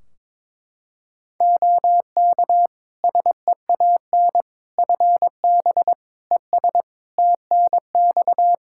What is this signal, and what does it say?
Morse code, heard as a keyed tone, about 22 wpm, 700 Hz. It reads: OK SEAN FB ES TNX
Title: Practice QSO - Exchange 2 - Info - 22wpm